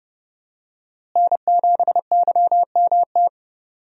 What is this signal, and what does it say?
Morse code, heard as a keyed tone, about 30 wpm, 700 Hz. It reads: N7YMT
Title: USA Call Signs - 5 Characters - Repeated 2x - 30wpm